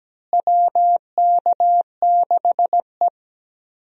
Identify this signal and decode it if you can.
Morse code, heard as a keyed tone, about 17 wpm, 700 Hz. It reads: WK6E
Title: USA Call Signs - 4 Characters - Repeated 3x - 17wpm